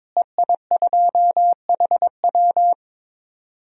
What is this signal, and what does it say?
Morse code, heard as a keyed tone, about 22 wpm, 700 Hz. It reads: EI2HW
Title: DXCC-100 Call Signs - 5 Characters - Repeated 3x - 22wpm